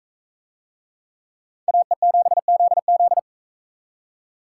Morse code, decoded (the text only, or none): AE7ZZ